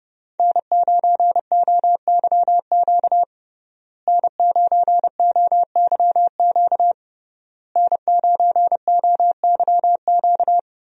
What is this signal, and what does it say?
Morse code, heard as a keyed tone, about 30 wpm, 700 Hz. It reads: N9OYQ N9OYQ N9OYQ
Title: USA Call Signs - 5 Characters - Repeated 3x - 30wpm